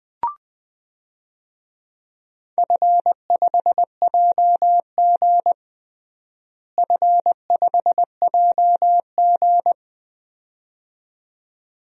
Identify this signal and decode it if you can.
Morse code, heard as a keyed tone, about 20 wpm, 700 Hz. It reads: F5JG F5JG
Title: DXCC-100 Call Signs - 4 Characters - Repeated 2x - 20wpm